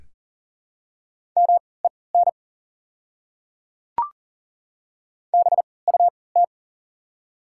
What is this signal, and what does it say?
Morse code, heard as a keyed tone, about 38 wpm, spaced as for 22 wpm, 690 Hz. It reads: MEN BUT
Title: 3x Character Spacing - Top 200 Words - 40wpm